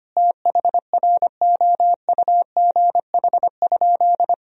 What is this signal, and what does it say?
Morse code, heard as a keyed tone, about 25 wpm, 700 Hz. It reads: THROUGH?
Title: Sentences from Top 1000 Words - Review Part 2 - 25wpm